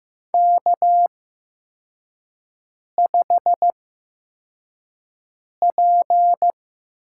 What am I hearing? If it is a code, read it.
Morse code, heard as a keyed tone, about 15 wpm, spaced as for 4 wpm, 700 Hz. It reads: K5P
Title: USA Call Signs - 5 Characters - 8x Character Spacing - 15wpm